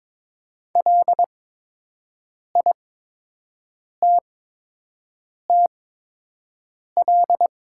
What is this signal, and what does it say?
Morse code, heard as a keyed tone, about 22 wpm, spaced as for 6 wpm, 700 Hz. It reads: LITTL
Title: 8x Character Spacing - Top 500 Words - 22wpm